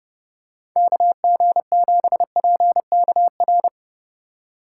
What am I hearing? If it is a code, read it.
Morse code, heard as a keyed tone, about 32 wpm, 700 Hz. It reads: KG7PKR
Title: USA Call Signs - 6 Characters - 3x Character Spacing - 30wpm